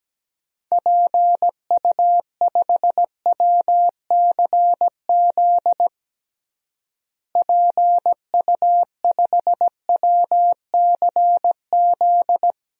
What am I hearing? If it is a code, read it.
Morse code, heard as a keyed tone, about 17 wpm, 700 Hz. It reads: PU5WCZ PU5WCZ
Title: DXCC-100 Call Signs - 6 Characters - Repeated 3x - 17wpm